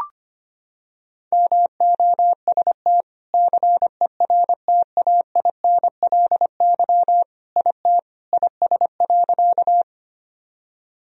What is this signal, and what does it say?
Morse code, heard as a keyed tone, about 25 wpm, 700 Hz. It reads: MOST CERTAINLY IT IS.
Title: Sentences from Top 1000 Words - Review Part 1 - 25wpm